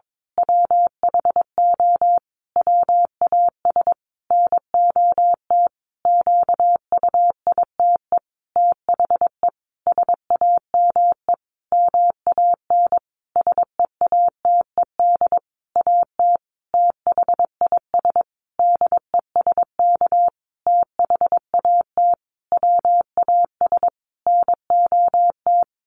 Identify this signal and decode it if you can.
Morse code, heard as a keyed tone, about 22 wpm, 700 Hz. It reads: WHO WAS NOT QUITE THE SAME MAN SEATED AT THIS DESK THAT WAS NOT
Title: A Sound Of Thunder 22wpm